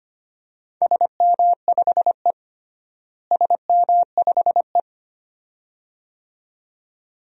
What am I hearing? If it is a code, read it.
Morse code, heard as a keyed tone, about 25 wpm, 700 Hz. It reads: SM5E SM5E
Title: DXCC-100 Call Signs - 4 Characters - Repeated 2x - 25wpm